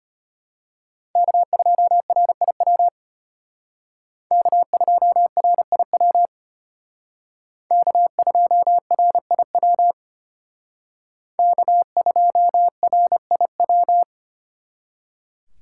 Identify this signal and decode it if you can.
Morse code, heard as a keyed tone, about 31 wpm, 700 Hz. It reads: K2RIW K2RIW K2RIW K2RIW
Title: Speed Racing - USA Call Signs 25wpm